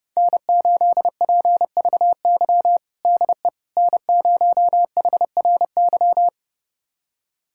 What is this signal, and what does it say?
Morse code, heard as a keyed tone, about 30 wpm, 700 Hz. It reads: N8PVY DE N0HRY
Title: USA Callsign DE Callsign - 5 Characters - 30wpm